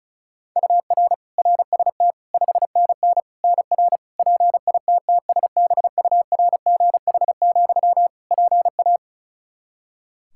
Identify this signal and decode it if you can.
Morse code, heard as a keyed tone, about 34 wpm, 700 Hz. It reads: UR RST 5NN NR PITTSBURGH, PA
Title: Practice QSO - Exchange 1 - Beginning - 35wpm